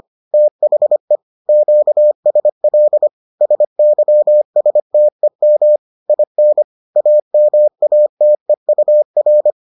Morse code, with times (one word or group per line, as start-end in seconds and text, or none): 0.3-1.2 THE
1.5-3.1 QSL
3.4-5.8 SYSTEM
6.1-6.6 IN
7.0-9.5 AMATEUR